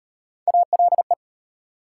Morse code, heard as a keyed tone, about 36 wpm, 700 Hz. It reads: ALE